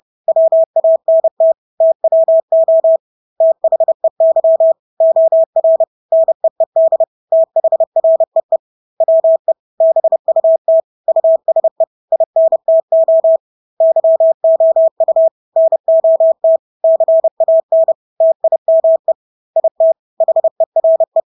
Morse code, heard as a keyed tone, about 30 wpm, 650 Hz. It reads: WANT TWO THEY OR NEED THREE WE BUT USE INTO YOU NOT CAN TIME IT HERE
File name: Engelske ord 30-30 wpm